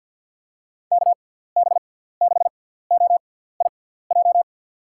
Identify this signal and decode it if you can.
Morse code, heard as a keyed tone, about 51 wpm, spaced as for 17 wpm, 700 Hz. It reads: KB6OIJ
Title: USA Call Signs - 6x Character Spacing - 50wpm